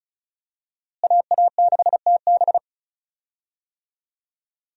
Morse code, heard as a keyed tone, about 36 wpm, 700 Hz. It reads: AA6TB
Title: USA Call Signs - 5 Characters - Repeated 2x - 35wpm